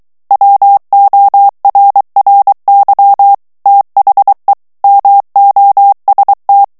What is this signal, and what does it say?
Morse code, heard as a keyed tone, about 23 wpm, 800 Hz. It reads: WORRY THE MOST
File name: morse mp3 / ANGLAIS texte clair 1 vitesse 22 mots minute